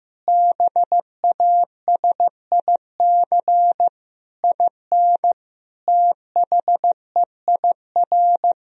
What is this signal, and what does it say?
Morse code, heard as a keyed tone, about 16 wpm, 700 Hz. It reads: BASIC IN THEIR
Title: Sentences from Top 1000 Words - Review Part 3 - 15wpm